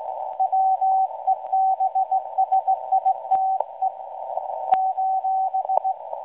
Morse code, heard as a keyed tone, about 18 wpm, 760 Hz. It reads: WEBSITE :
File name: WebSDR records / SAQ at PI4THT msg 0800-0804 17.2kHz